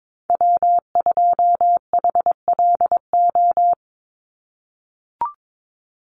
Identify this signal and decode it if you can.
Morse code, heard as a keyed tone, about 22 wpm, 700 Hz. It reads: W2HLO
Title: USA Call Signs 22wpm